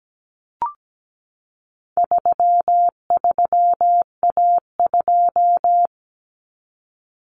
Morse code, heard as a keyed tone, about 17 wpm, 700 Hz. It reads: 33A2
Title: Common Cut Numbers - 4-Digit 17wpm